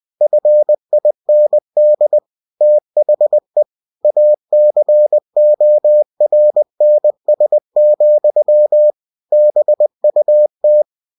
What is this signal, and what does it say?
Morse code, heard as a keyed tone, about 20 wpm, 600 Hz. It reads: FIND THE ACORNS, BUT
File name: cw-books / winnie-the-pooh / winnie-20wpm-page-0037